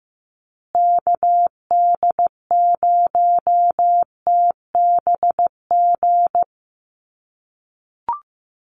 Morse code, heard as a keyed tone, about 15 wpm, 700 Hz. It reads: KD0TBG
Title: USA Call Signs Repeated 3x 15wpm